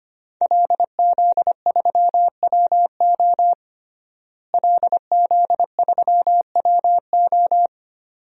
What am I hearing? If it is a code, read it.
Morse code, heard as a keyed tone, about 25 wpm, 700 Hz. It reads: LZ3WO LZ3WO
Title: DXCC-100 Call Signs - 5 Characters - Repeated 2x - 25wpm